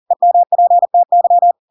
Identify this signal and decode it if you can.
Morse code, heard as a keyed tone, about 40 wpm, 700 Hz. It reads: EMPTY
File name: Pinocchio / Pinocchio ch5 40 40 ik1hge